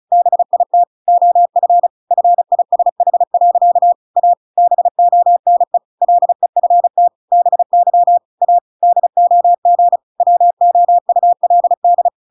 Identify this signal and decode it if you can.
Morse code, heard as a keyed tone, about 34 wpm, 700 Hz. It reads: BIT OF FISH. A BONE LEFT BY A DOG WOULD